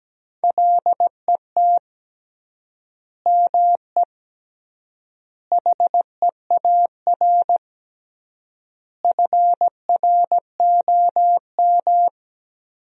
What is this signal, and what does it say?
Morse code, heard as a keyed tone, about 17 wpm, 700 Hz. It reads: LET ME HEAR FROM
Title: 3x Word-Spacing - Sentences from Top 400 Words - 17wpm